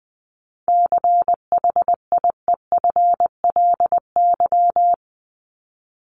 Morse code, heard as a keyed tone, about 20 wpm, 700 Hz. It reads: CHIEFLY
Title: LY Suffix - Top 100 Words with Base Words 20wpm